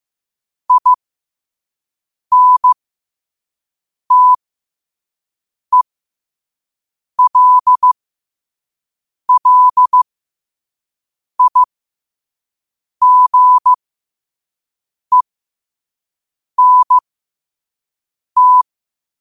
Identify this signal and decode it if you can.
Morse code, heard as a keyed tone, about 15 wpm, spaced as for 5 wpm, 1000 Hz. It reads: INTELLIGENT
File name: qotdfiles / 5wpm